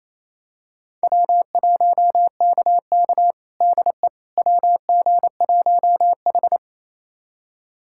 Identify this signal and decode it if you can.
Morse code, heard as a keyed tone, about 28 wpm, 700 Hz. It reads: W1KK DE WG1H
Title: USA Callsign DE Callsign - 4 Characters - 28wpm